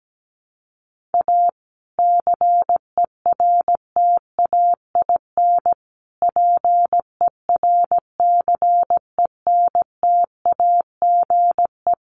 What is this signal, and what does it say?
Morse code, heard as a keyed tone, about 17 wpm, 700 Hz. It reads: A CERTAIN PERCENTAGE
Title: Sentences from Top 1300 Words 17wpm